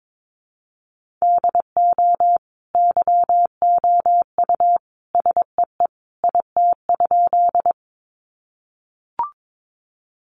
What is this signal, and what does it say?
Morse code, heard as a keyed tone, about 22 wpm, 700 Hz. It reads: DO YOU SEE IT?
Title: Sentences from Top 100 Words 22wpm